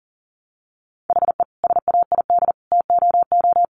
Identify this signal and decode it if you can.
Morse code, heard as a keyed tone, about 40 wpm, 700 Hz. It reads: HE SAID TOO
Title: Sentences from Top 200 Words 40wpm